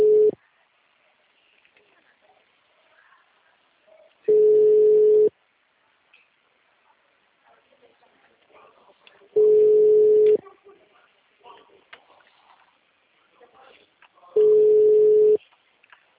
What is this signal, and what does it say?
Ringback tone (Europe, 425 Hz), with speech.
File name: out- -2007- - - .49002